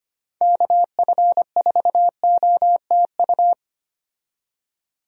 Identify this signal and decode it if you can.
Morse code, heard as a keyed tone, about 25 wpm, 700 Hz. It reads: KF4OTU
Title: USA Call Signs - 6 Characters - 25wpm